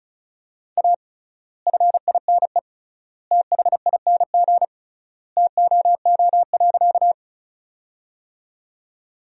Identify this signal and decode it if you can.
Morse code, heard as a keyed tone, about 34 wpm, 700 Hz. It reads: A FINE THING TOO.
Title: 3x Word-Spacing - Sentences from Top 500 Words - 35wpm